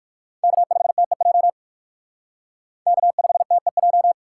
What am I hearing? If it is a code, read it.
Morse code, heard as a keyed tone, about 56 wpm, 700 Hz. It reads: K5TEJ K5TEJ
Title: Speed Racing - USA 1x3 Call Signs 45wpm